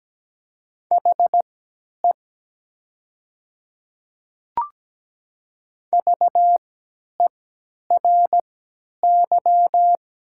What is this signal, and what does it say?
Morse code, heard as a keyed tone, about 17 wpm, spaced as for 10 wpm, 700 Hz. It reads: HE VERY
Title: 3x Character Spacing - Top 200 Words - 17wpm